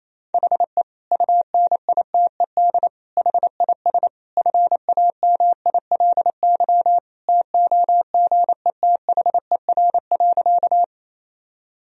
Morse code, heard as a keyed tone, about 28 wpm, 700 Hz. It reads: HE UNITED HIS FAMILY TOGETHER.
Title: Sentences from Top 1000 Words - Review Part 4 - 28wpm